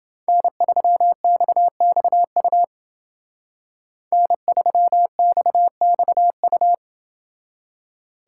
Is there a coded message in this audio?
Yes: Morse code from 0.3 to 6.7 s, heard as a keyed tone, about 28 wpm, 700 Hz. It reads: N3XXU N3XXU